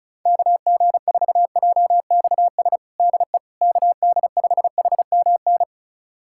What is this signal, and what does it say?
Morse code, heard as a keyed tone, about 34 wpm, 700 Hz. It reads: KG4JXS DE KD5HMN
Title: USA Callsign DE Callsign - 6 Characters - 35wpm